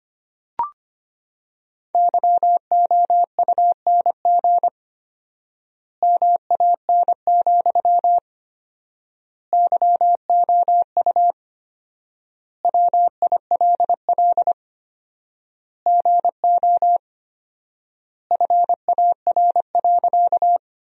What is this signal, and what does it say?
Morse code, heard as a keyed tone, about 25 wpm, 700 Hz. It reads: YOUNG MAN, YOU WILL GO FAR.
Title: 4x Word-Spacing - Sentences from Top 400 Words - 25wpm